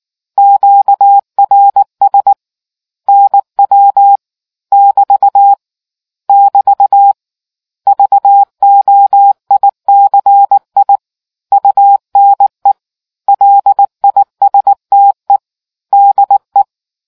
Morse code, heard as a keyed tone, about 19 wpm, 800 Hz. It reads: QRS NW <BT> <BT> VOICI UNE LISTE DE